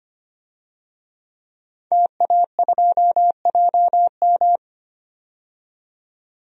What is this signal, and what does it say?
Morse code, heard as a keyed tone, about 25 wpm, 700 Hz. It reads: TA2JM
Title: DXCC-100 Call Signs - 5 Characters - 25wpm